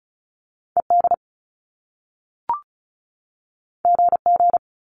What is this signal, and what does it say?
Morse code, heard as a keyed tone, about 36 wpm, 700 Hz. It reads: ED GG
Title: Top Two Letter Combinations in Words 35wpm